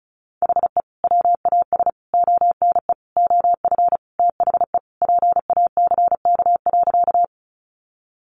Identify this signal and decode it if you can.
Morse code, heard as a keyed tone, about 34 wpm, 700 Hz. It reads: HE WAS ONE OF THE PACK.
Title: Sentences from Top 1300 Words 35wpm